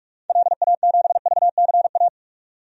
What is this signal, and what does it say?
Morse code, heard as a keyed tone, about 45 wpm, 700 Hz. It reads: LA7VCA
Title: DXCC-100 Call Signs - 6 Characters - 45wpm